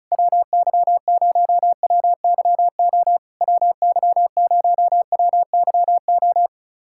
Morse code, heard as a keyed tone, about 36 wpm, 700 Hz. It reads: WY0WYO WY0WYO
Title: SOTA QSO Practice 35wpm